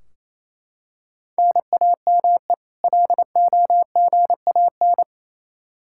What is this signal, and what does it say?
Morse code, heard as a keyed tone, about 27 wpm, 700 Hz. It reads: NAME LOGAN
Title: Practice QSO - Exchange 1 - Beginning - 28wpm